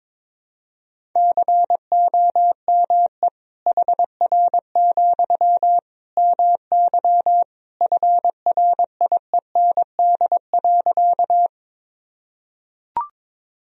Morse code, heard as a keyed tone, about 22 wpm, 700 Hz. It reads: COME HR, MY FRIEND.
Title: Sentences from Top 300 Words and Abbreviations 22wpm